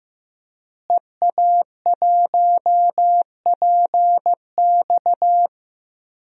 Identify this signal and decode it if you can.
Morse code, heard as a keyed tone, about 16 wpm, 700 Hz. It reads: EA1PX
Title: DXCC-100 Call Signs - 5 Characters - Repeated 2x - 15wpm